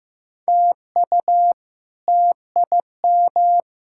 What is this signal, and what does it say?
Morse code, heard as a keyed tone, about 15 wpm, 700 Hz. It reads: TU TIM
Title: K1USN SST QSO Practice - 15wpm